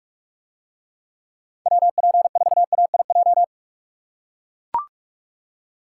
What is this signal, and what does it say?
Morse code, heard as a keyed tone, about 44 wpm, 700 Hz. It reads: WP4AIJ